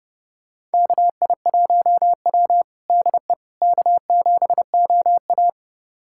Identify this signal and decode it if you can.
Morse code, heard as a keyed tone, about 32 wpm, 700 Hz. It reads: KI1W DE K7OA
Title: USA Callsign DE Callsign - 4 Characters - 30wpm